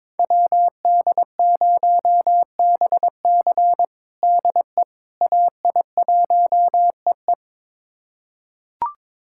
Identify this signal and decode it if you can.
Morse code, heard as a keyed tone, about 22 wpm, 700 Hz. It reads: WD0BC DE AI1EE
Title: USA Callsign DE Callsign - 5 Characters - 22wpm